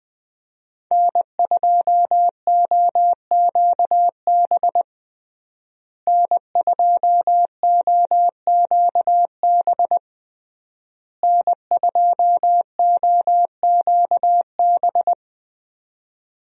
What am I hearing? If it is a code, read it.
Morse code, heard as a keyed tone, about 20 wpm, 700 Hz. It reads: N2OQB N2OQB N2OQB
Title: USA Call Signs - 5 Characters - Repeated 3x - 20wpm